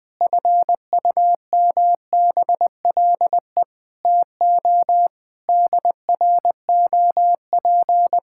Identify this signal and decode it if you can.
Morse code, heard as a keyed tone, about 20 wpm, 700 Hz. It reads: FUMBLE TO DROP
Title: Pangram Challenge v2 - 20wpm